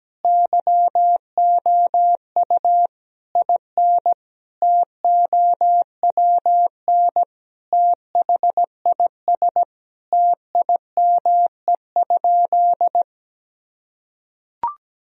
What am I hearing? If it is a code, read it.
Morse code, heard as a keyed tone, about 17 wpm, 700 Hz. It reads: YOU IN TOWN THIS TIME?
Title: Sentences from Top 1000 Words - Review Part 2 - 17wpm